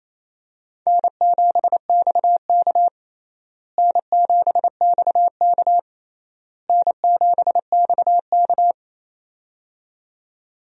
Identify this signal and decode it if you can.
Morse code, heard as a keyed tone, about 28 wpm, 700 Hz. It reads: N7XK N7XK N7XK
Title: USA Call Signs - 4 Characters - Repeated 3x - 28wpm